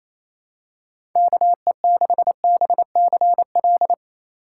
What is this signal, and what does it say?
Morse code, heard as a keyed tone, about 28 wpm, 700 Hz. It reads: KE6BCL
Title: USA Call Signs - 6 Characters - 4x Character Spacing - 28wpm